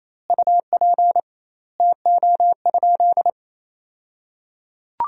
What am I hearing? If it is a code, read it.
Morse code, heard as a keyed tone, about 28 wpm, 700 Hz. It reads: UP TO?
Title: 2x Word-Spacing - Sentences from Top 500 Words - 28wpm